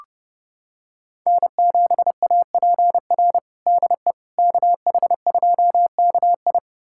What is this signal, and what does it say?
Morse code, heard as a keyed tone, about 30 wpm, 700 Hz. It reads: N7APR DE KH2KI
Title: USA Callsign DE Callsign - 5 Characters - 30wpm